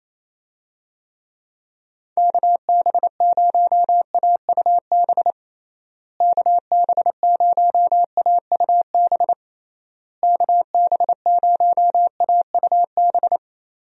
Morse code, heard as a keyed tone, about 28 wpm, 700 Hz. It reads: KB0AUB KB0AUB KB0AUB
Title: USA Call Signs - 6 Characters - Repeated 3x - 28wpm